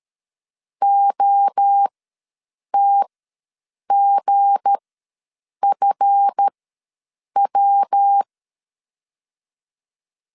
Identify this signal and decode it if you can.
Morse code, heard as a keyed tone, about 13 wpm, spaced as for 7 wpm, 800 Hz. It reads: OTGFW
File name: LEZIONI / 7 GF9Z / 7 GF ws 12wpm 3140